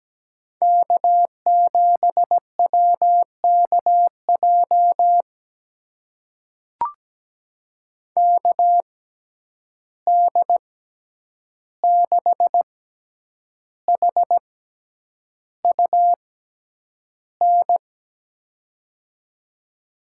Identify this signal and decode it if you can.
Morse code, heard as a keyed tone, about 17 wpm, spaced as for 8 wpm, 700 Hz. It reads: K7WKJ KD6HUN
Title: USA Call Signs - 6x Character Spacing - 17wpm